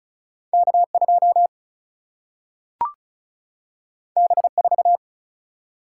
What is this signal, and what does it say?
Morse code, heard as a keyed tone, about 36 wpm, 700 Hz. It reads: K2 B4
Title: Single-Letter Single-Number 35wpm